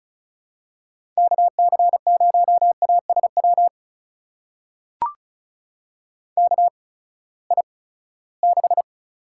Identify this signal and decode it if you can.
Morse code, heard as a keyed tone, about 36 wpm, spaced as for 20 wpm, 700 Hz. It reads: KC0ASW KI6
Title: USA Call Signs - 6 Characters - 8x Character Spacing - 35wpm